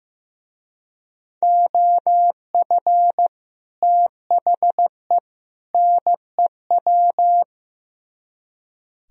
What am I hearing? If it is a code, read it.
Morse code, heard as a keyed tone, about 17 wpm, 700 Hz. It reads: OF THE NEW
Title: Speed-Racing - Sets of 3 Words 15wpm